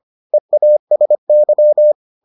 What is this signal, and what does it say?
Morse code, heard as a keyed tone, about 25 wpm, 600 Hz. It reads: EASY